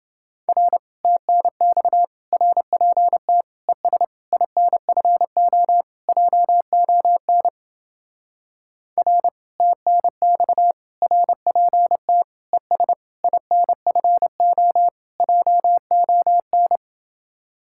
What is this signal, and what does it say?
Morse code, heard as a keyed tone, about 31 wpm, 700 Hz. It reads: R TNX RPT ES INFO JON R TNX RPT ES INFO JON